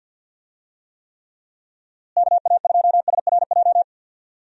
Morse code, heard as a keyed tone, about 50 wpm, 700 Hz. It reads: KA2SRJ
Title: USA Call Signs - 6 Characters - Repeated 3x - 50wpm